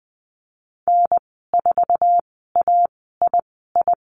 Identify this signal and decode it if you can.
Morse code, heard as a keyed tone, about 20 wpm, spaced as for 15 wpm, 700 Hz. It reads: N4AII
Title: 048 Lesson 9g - Number 4 - Call Signs 20wpm